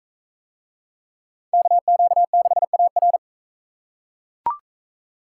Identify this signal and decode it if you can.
Morse code, heard as a keyed tone, about 41 wpm, 700 Hz. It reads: KQ6AR